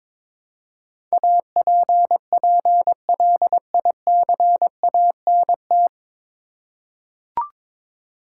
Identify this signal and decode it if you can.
Morse code, heard as a keyed tone, about 22 wpm, 700 Hz. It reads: APPLICANT
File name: 9 Letter Words - First 3 Letters Given - 22wpm